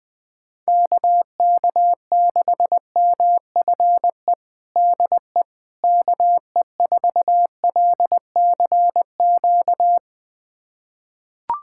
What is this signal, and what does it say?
Morse code, heard as a keyed tone, about 21 wpm, 700 Hz. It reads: KK6MFE DE KE4LCQ
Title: USA Callsign DE Callsign - 6 Characters - 20wpm